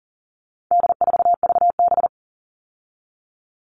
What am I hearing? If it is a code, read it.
Morse code, heard as a keyed tone, about 40 wpm, 700 Hz. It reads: D4VB
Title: All Cut Numbers - 4-Digit 40wpm